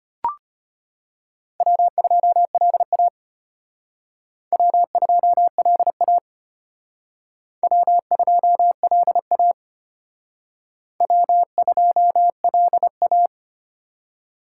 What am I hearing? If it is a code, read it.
Morse code, heard as a keyed tone, about 34 wpm, 700 Hz. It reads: W2LA W2LA W2LA W2LA